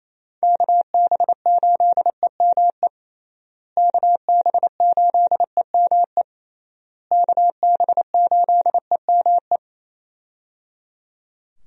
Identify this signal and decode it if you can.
Morse code, heard as a keyed tone, about 28 wpm, 700 Hz. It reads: KB8EME KB8EME KB8EME
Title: USA Call Signs - 6 Characters - Repeated 3x - 28wpm